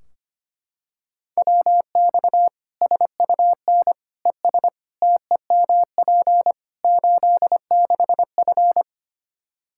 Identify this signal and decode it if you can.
Morse code, heard as a keyed tone, about 25 wpm, 700 Hz. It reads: WX SUN ES TEMP 86F
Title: K1USN SST QSO Practice 25wpm